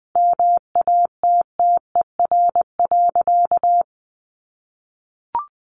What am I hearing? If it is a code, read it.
Morse code, heard as a keyed tone, about 20 wpm, 700 Hz. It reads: MATTER.
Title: Sentences from Top 1200 Words 20wpm